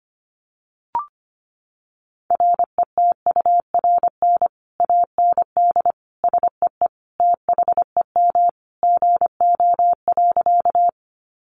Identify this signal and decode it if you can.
Morse code, heard as a keyed tone, about 25 wpm, 700 Hz. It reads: RETURN AND SEE THEM GO.
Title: Sentences from Top 800 Words 25wpm